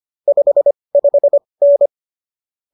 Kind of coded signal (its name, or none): Morse code